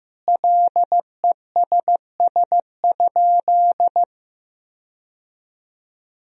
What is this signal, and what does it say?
Morse code, heard as a keyed tone, about 16 wpm, 700 Hz. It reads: LESS?
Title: Sentences from Top 1000 Words - Review Part 2 - 15wpm